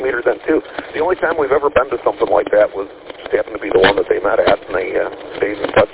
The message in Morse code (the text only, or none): none